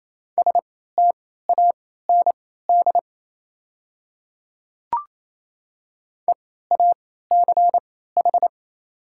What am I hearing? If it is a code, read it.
Morse code, heard as a keyed tone, about 27 wpm, spaced as for 16 wpm, 700 Hz. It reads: STAND EACH
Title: 3x Character Spacing - Top 300 Words - 28wpm